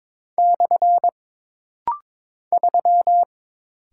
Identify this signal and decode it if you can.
Morse code, heard as a keyed tone, about 22 wpm, 700 Hz. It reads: / 3
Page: This rapid-fire timing is geared to push your ICR (Instant Character Recognition) skills.